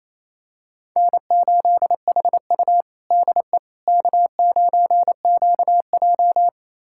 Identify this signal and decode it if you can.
Morse code, heard as a keyed tone, about 28 wpm, 700 Hz. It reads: N8HU DE K9QJ
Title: USA Callsign DE Callsign - 4 Characters - 28wpm